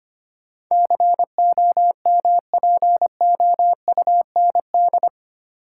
Morse code, heard as a keyed tone, about 25 wpm, 690 Hz. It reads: COMPOUND
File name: 8 Letter Words - First 3 Letters Given - 25wpm